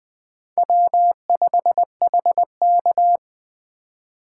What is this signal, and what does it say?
Morse code, heard as a keyed tone, about 20 wpm, 700 Hz. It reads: W5HK
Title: Speed Racing - USA 1x2 Call Signs 17wpm